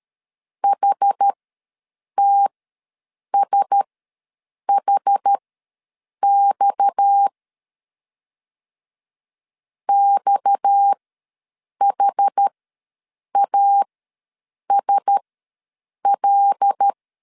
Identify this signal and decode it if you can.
Morse code, heard as a keyed tone, about 13 wpm, spaced as for 7 wpm, 800 Hz. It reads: HTSHX XHASL
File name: LEZIONI / 6 PKX0 / 6 PKX ws 12wpm 3106